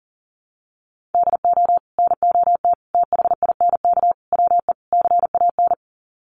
Morse code, heard as a keyed tone, about 40 wpm, 700 Hz. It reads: DO NOT THINK WE CAN
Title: Sets of 5 Words Encore2 40wpm